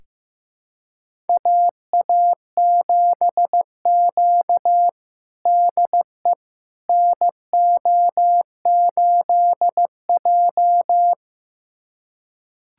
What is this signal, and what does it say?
Morse code, heard as a keyed tone, about 15 wpm, 700 Hz. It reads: AA7Q DE NO8J
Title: USA Callsign DE Callsign - 4 Characters - 15wpm